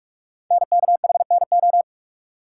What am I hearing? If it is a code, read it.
Morse code, heard as a keyed tone, about 44 wpm, 690 Hz. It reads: NKHNO
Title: Code Groups - 5 Letters - 45wpm